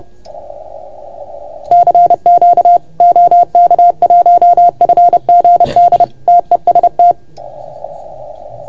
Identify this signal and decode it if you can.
Morse code, heard as a keyed tone, about 30 wpm, 690 Hz. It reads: CQ OK1FGD TEST
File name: Archiv / CUC381 / OK1FGD CUC 381